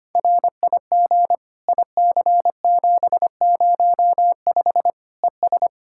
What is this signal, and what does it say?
Morse code, heard as a keyed tone, about 25 wpm, 700 Hz. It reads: RIG IC705 ES
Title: Practice QSO - Exchange 2 - Info - 25wpm